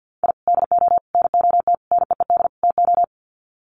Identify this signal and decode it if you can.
Morse code, heard as a keyed tone, about 50 wpm, 700 Hz. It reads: I DO NOT NEED TO
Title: Sets of 5 Words 50wpm